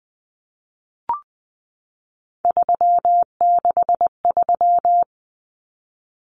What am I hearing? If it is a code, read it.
Morse code, heard as a keyed tone, about 20 wpm, 700 Hz. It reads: 363
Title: Common Cut Numbers - 3-Digit 20wpm